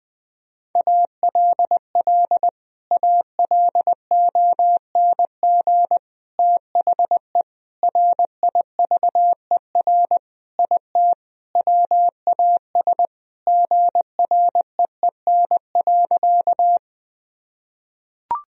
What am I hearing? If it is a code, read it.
Morse code, heard as a keyed tone, about 20 wpm, 700 Hz. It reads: ALL ALONG THE RIVER IT WAS GREEN.
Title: Sentences from Top 1000 Words - Review Part 2 - 20wpm